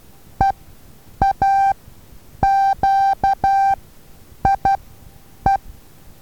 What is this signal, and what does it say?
Morse code, heard as a keyed tone, about 12 wpm, spaced as for 8 wpm, 780 Hz. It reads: EAQIE
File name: Taviro HA6VA A